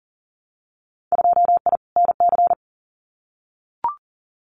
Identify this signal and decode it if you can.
Morse code, heard as a keyed tone, about 37 wpm, 700 Hz. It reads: 2I NC
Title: Winter Field Day Exchanges 40wpm